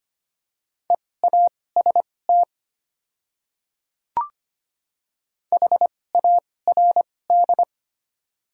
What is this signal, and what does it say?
Morse code, heard as a keyed tone, about 25 wpm, spaced as for 18 wpm, 700 Hz. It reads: EAST HARD
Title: 2x Character Spacing - Top 500 Words - 25wpm